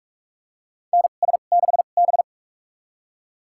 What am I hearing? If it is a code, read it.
Morse code, heard as a keyed tone, about 41 wpm, spaced as for 31 wpm, 700 Hz. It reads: NS6B
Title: USA Call Signs - 4 Characters - 2x Character Spacing - 45wpm